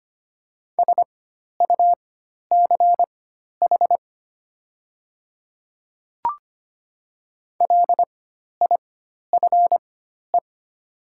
Morse code, heard as a keyed tone, about 25 wpm, spaced as for 12 wpm, 700 Hz. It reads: SUCH LIFE
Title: 4x Character Spacing - Top 300 Words - 25wpm